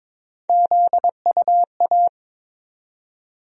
Morse code, heard as a keyed tone, about 21 wpm, 700 Hz. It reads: ZUA
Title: USA Call Sign Postfixes - 3 Letters 22wpm